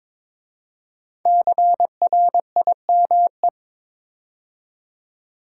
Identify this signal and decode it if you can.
Morse code, heard as a keyed tone, about 22 wpm, 690 Hz. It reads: CRIME
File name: 5 Letter Words - First 3 Letters Given - 22wpm